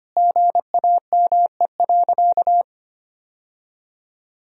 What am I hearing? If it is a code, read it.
Morse code, heard as a keyed tone, about 25 wpm, 700 Hz. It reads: GAME.
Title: Sentences from Top 1000 Words - Review Part 3 - 25wpm